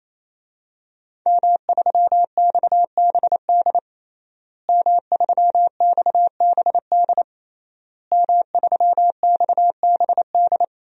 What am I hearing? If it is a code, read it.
Morse code, heard as a keyed tone, about 28 wpm, 700 Hz. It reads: M3XBD M3XBD M3XBD
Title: DXCC-100 Call Signs - 5 Characters - Repeated 3x - 28wpm